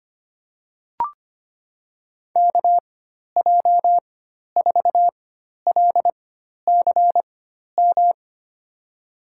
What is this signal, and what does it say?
Morse code, heard as a keyed tone, about 25 wpm, spaced as for 12 wpm, 700 Hz. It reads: KJ4LCM
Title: USA Call Signs - 6 Characters - 4x Character Spacing - 25wpm